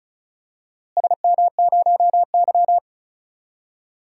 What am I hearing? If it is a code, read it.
Morse code, heard as a keyed tone, about 34 wpm, 700 Hz. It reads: SM0Y
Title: DXCC-100 Call Signs - 4 Characters - Repeated 2x - 35wpm